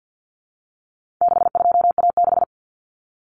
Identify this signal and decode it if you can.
Morse code, heard as a keyed tone, about 50 wpm, 700 Hz. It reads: 62A6
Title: Common Cut Numbers - 4-Digit 50wpm